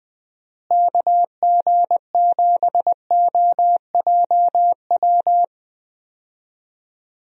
Morse code, heard as a keyed tone, about 20 wpm, 700 Hz. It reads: KG7OJW
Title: USA Call Signs - 6 Characters - Repeated 2x - 20wpm